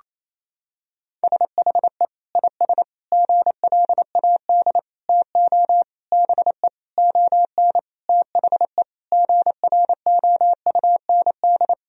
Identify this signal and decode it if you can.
Morse code, heard as a keyed tone, about 27 wpm, 700 Hz. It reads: SHE IS GLAD TO BE ON THE GROUND
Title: Sentences from Top 1000 Words - Review Part 2 - 28wpm